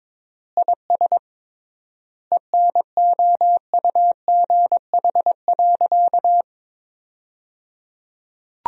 Morse code, heard as a keyed tone, about 22 wpm, 700 Hz. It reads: IS ENOUGH.
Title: 3x Word-Spacing - Sentences from Top 400 Words - 22wpm